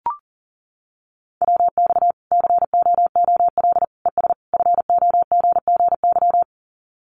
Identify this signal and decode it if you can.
Morse code, heard as a keyed tone, about 40 wpm, 700 Hz. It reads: WX COOL ES FOGGY
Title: Common QSO Phrases 40wpm